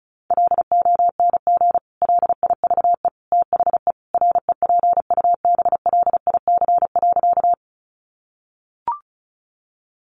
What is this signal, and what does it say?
Morse code, heard as a keyed tone, about 34 wpm, 700 Hz. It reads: LONG LIVE THE REPUBLIC.
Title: Sentences from Top 1500 Words 35wpm